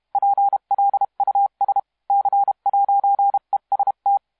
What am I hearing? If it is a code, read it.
Morse code, heard as a keyed tone, about 32 wpm, 800 Hz. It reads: PLUS C'EST
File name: musiqueradio / cw hsc demo f6ddr